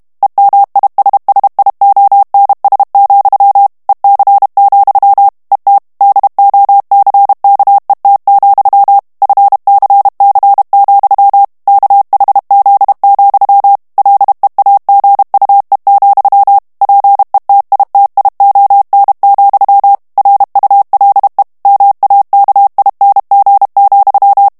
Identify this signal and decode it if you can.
Morse code, heard as a keyed tone, about 32 wpm, 800 Hz. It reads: EMISSIONS, EC, ET DOCKET, FCC, KHZ, LEAGUE, PETITION, RULE MAKING,
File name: ANGLAIS texte clair 3 vitesse 30 mots minute